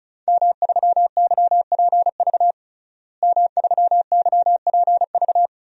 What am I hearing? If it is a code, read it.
Morse code, heard as a keyed tone, about 34 wpm, 700 Hz. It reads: M3YPV M3YPV
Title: DXCC-100 Call Signs - 5 Characters - Repeated 2x - 35wpm